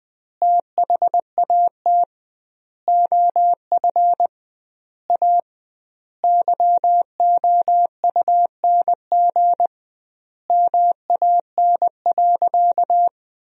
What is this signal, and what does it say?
Morse code, heard as a keyed tone, about 20 wpm, 700 Hz. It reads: THAT OF A YOUNG MAN.
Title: 2x Word-Spacing - Sentences from Top 500 Words - 20wpm